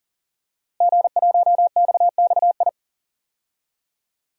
Morse code, heard as a keyed tone, about 40 wpm, 700 Hz. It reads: G1XXI
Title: DXCC-100 Call Signs - 5 Characters - Repeated 2x - 40wpm